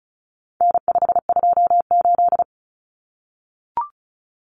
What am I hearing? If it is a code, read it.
Morse code, heard as a keyed tone, about 34 wpm, 700 Hz. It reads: N528
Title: Common Cut Numbers - 4-Digit 35wpm